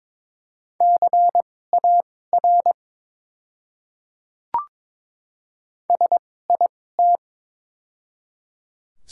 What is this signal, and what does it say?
Morse code, heard as a keyed tone, about 22 wpm, spaced as for 16 wpm, 690 Hz. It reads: CAR SIT
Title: 2x Character Spacing - Top 500 Words - 22wpm